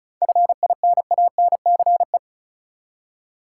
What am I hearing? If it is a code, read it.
Morse code, heard as a keyed tone, about 34 wpm, 700 Hz. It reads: FINANCE
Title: Seven Letter Words - 35wpm